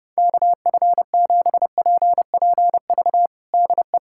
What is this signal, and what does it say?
Morse code, heard as a keyed tone, about 30 wpm, 700 Hz. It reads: KF7PPV DE
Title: USA Callsign DE Callsign 30wpm